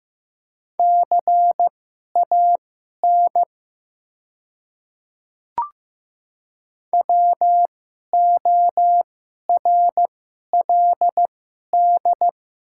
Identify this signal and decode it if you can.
Morse code, heard as a keyed tone, about 15 wpm, spaced as for 11 wpm, 700 Hz. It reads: CAN WORLD
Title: 2x Character Spacing - Top 300 Words - 15wpm